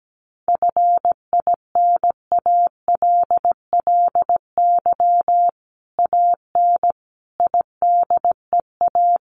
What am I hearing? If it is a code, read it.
Morse code, heard as a keyed tone, about 17 wpm, 700 Hz. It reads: FINALLY AN IDEA
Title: Sentences from Top 1000 Words 17wpm